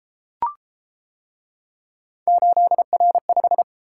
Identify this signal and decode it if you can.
Morse code, heard as a keyed tone, about 32 wpm, 700 Hz. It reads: 8R5